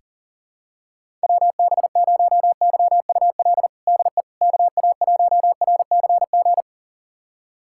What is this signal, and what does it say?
Morse code, heard as a keyed tone, about 40 wpm, 700 Hz. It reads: WB0YUL DE KA1RCG
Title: USA Callsign DE Callsign 40wpm